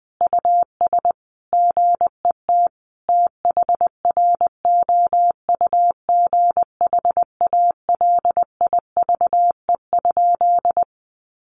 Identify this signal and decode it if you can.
Morse code, heard as a keyed tone, about 20 wpm, 700 Hz. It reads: US GET THROUGH ALIVE?
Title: A Sound Of Thunder 20wpm